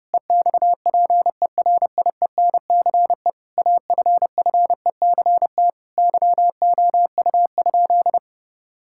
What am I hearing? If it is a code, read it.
Morse code, heard as a keyed tone, about 31 wpm, 700 Hz. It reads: EXPERIENCE AFFECT YOU?
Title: Sentences from Top 1000 Words - Review Part 6 - 30wpm